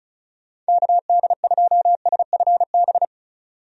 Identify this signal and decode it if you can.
Morse code, heard as a keyed tone, about 34 wpm, 700 Hz. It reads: KD2SFB
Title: USA Call Signs - 6 Characters - 35wpm